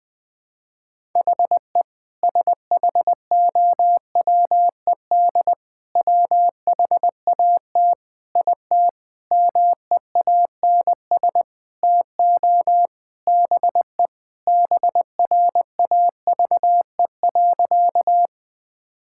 Morse code, heard as a keyed tone, about 20 wpm, 700 Hz. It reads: HE SHOWED WHAT IT MEANS TO BE BRAVE.
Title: Sentences with 5-Word Phrases - Spoken Beginning - 20wpm